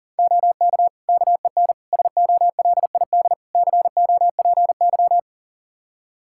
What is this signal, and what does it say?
Morse code, heard as a keyed tone, about 40 wpm, 700 Hz. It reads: OK KEN SOLID COPY
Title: Practice QSO - Exchange 3 - More Info - 40wpm